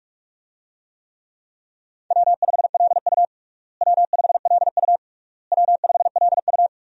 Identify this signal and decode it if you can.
Morse code, heard as a keyed tone, about 46 wpm, 700 Hz. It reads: W5LU W5LU W5LU
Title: USA Call Signs - 4 Characters - Repeated 3x - 45wpm